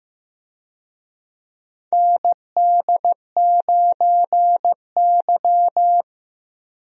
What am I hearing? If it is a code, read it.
Morse code, heard as a keyed tone, about 15 wpm, 700 Hz. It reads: ND9Y
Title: USA Call Signs - 4 Characters - Repeated 3x - 15wpm